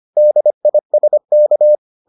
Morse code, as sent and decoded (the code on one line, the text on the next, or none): -.. .. ... -.-
DISK